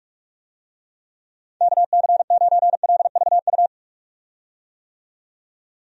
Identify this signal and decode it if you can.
Morse code, heard as a keyed tone, about 46 wpm, 700 Hz. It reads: KC9LVU
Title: USA Call Signs - 6 Characters - 45wpm